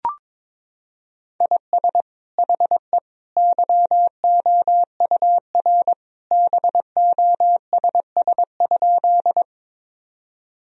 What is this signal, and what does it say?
Morse code, heard as a keyed tone, about 21 wpm, 700 Hz. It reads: IS HE YOUR BOSS?
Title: Sentences from Top 1000 Words - Review Part 4 - 22wpm